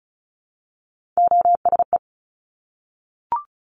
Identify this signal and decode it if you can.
Morse code, heard as a keyed tone, about 36 wpm, 700 Hz. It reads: OSE
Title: Top Three Letter Combinations in Words 35wpm